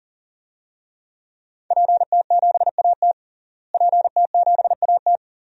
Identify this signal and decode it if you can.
Morse code, heard as a keyed tone, about 40 wpm, 700 Hz. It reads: PT7AT PT7AT
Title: DXCC-100 Call Signs - 5 Characters - Repeated 2x - 40wpm